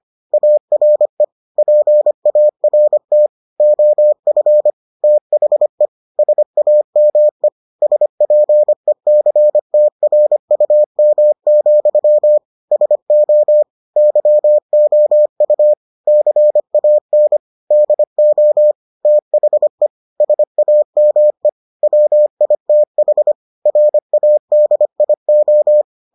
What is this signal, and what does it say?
Morse code, heard as a keyed tone, about 25 wpm, 600 Hz. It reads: ARE PART OF THE SAME SPECTRUM, SO YOU CAN DO THE SAME WITH RADIO